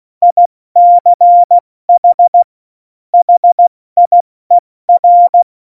Morse code, heard as a keyed tone, about 16 wpm, 700 Hz. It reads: ICH HIER
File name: Morsekurs / Witze / Witz 09. Januar 2018, 70 F80